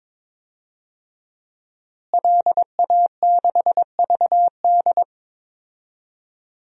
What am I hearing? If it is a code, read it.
Morse code, heard as a keyed tone, about 22 wpm, 700 Hz. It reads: LA6VD
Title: DXCC-100 Call Signs - 5 Characters - Repeated 3x - 22wpm